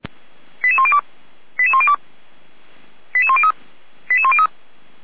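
Five-tone selective call (ZVEI), heard as quick runs of five tones, tones of about 70 ms. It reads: 88102 88102 88103 88103
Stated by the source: ZVEI five-tone selective call received by SDR.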